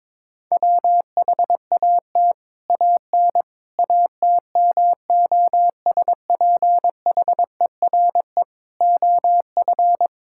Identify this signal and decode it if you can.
Morse code, heard as a keyed tone, about 22 wpm, 700 Hz. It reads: WHAT AN ATMOSPHERE OF
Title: Sentences from Top 1600 Words 22wpm